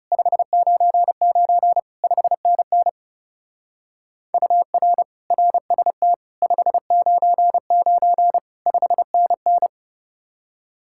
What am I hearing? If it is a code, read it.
Morse code, heard as a keyed tone, about 37 wpm, 700 Hz. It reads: 599 5NN UR RST 599 5NN
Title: Speed-Racing Common QSO Phrases - 30wpm